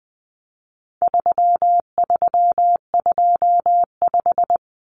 Morse code, heard as a keyed tone, about 20 wpm, 700 Hz. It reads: 3325
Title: Common Cut Numbers - 4-Digit 20wpm